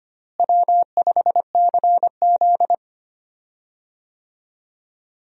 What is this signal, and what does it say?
Morse code, heard as a keyed tone, about 25 wpm, 700 Hz. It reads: W5CZ
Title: USA Call Signs - 4 Characters - Repeated 2x - 25wpm